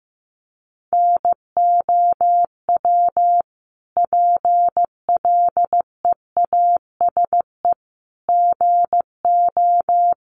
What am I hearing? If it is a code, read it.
Morse code, heard as a keyed tone, about 15 wpm, 700 Hz. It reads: NOW PLEASE GO
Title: Sentences from Top 1500 Words 15wpm